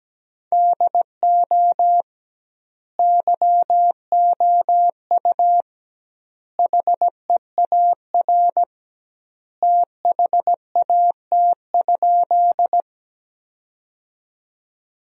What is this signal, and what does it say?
Morse code, heard as a keyed tone, about 17 wpm, 700 Hz. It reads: DO YOU HEAR THAT?
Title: 2x Word-Spacing - Sentences from Top 300 Words - 17wpm